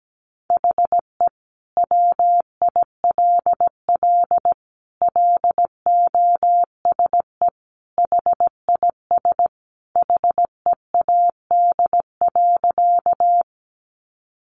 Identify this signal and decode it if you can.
Morse code, heard as a keyed tone, about 17 wpm, 700 Hz. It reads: HE WILL LOSE HIS HEAD.
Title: Sentences from Top 1100 Words 17wpm